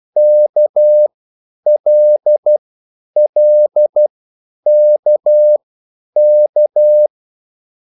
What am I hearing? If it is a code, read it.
Morse code, heard as a keyed tone, about 12 wpm, spaced as for 9 wpm, 600 Hz. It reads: KLLKK